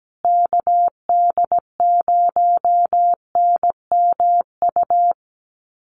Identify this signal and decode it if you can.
Morse code, heard as a keyed tone, about 17 wpm, 700 Hz. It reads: KD0NMU
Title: USA Call Signs Repeated 2x 17wpm